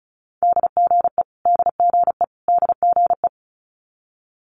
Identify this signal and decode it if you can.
Morse code, heard as a keyed tone, about 36 wpm, 700 Hz. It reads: DGE DGE DGE
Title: Multi-letter Phonemes Repeated 3x 35wpm